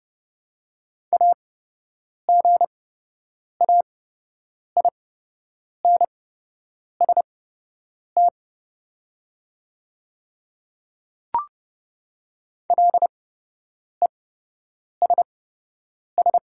Morse code, heard as a keyed tone, about 31 wpm, spaced as for 8 wpm, 700 Hz. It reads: AGAINST LESS
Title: 8x Character Spacing - Top 500 Words - 30wpm